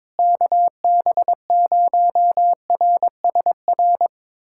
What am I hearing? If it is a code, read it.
Morse code, heard as a keyed tone, about 22 wpm, 700 Hz. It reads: KB0RSR
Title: USA Callsign DE Callsign - 6 Characters - 22wpm